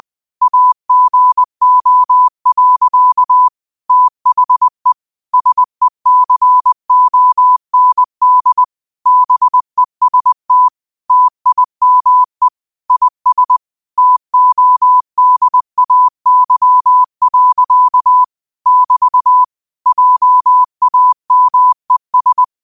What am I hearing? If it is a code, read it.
Morse code, heard as a keyed tone, about 20 wpm, 1000 Hz. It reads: AGO. THE SECOND BEST TIME IS TODAY. <BT> JAMES
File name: qotdfiles / 20wpm